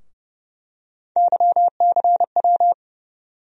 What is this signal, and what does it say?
Morse code, heard as a keyed tone, about 30 wpm, 700 Hz. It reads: YCW
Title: USA Call Sign Postfixes - 3 Letters 30wpm